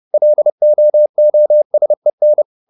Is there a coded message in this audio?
Yes: Morse code from 0.1 to 2.4 s, heard as a keyed tone, about 30 wpm, 600 Hz. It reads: LOOSEN